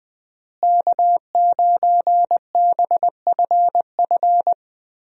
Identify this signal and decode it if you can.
Morse code, heard as a keyed tone, about 21 wpm, 700 Hz. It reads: K9BFF
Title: USA Call Signs - 5 Characters - Repeated 2x - 20wpm